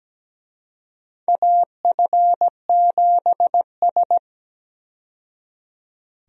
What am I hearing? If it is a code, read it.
Morse code, heard as a keyed tone, about 17 wpm, 700 Hz. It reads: AF7S
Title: USA Call Signs - 4 Characters - 17wpm